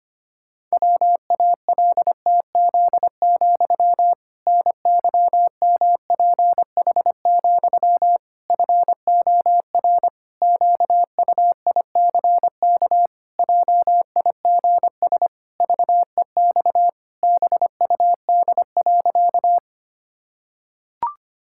Morse code, heard as a keyed tone, about 25 wpm, 700 Hz. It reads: WALTZ, NYMPH, FOR QUICK JIGS VEX BUD.
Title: Pangram Challenge v2 - 25wpm